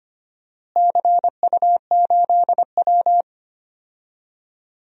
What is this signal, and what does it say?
Morse code, heard as a keyed tone, about 25 wpm, 700 Hz. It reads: CU8W
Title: DXCC-100 Call Signs - 4 Characters - Repeated 3x - 25wpm